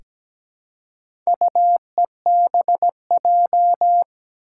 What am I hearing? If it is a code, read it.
Morse code, heard as a keyed tone, about 17 wpm, 700 Hz. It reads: UEBJ
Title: Four Challenging Letter-Number Combinations 17wpm